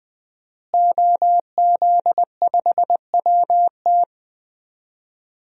Morse code, heard as a keyed tone, about 20 wpm, 700 Hz. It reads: OZ5WT
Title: DXCC-100 Call Signs - 5 Characters - Repeated 3x - 20wpm